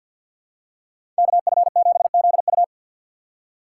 Morse code, heard as a keyed tone, about 52 wpm, 700 Hz. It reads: KF7ZU
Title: USA Call Signs - 5 Characters - 6x Character Spacing - 50wpm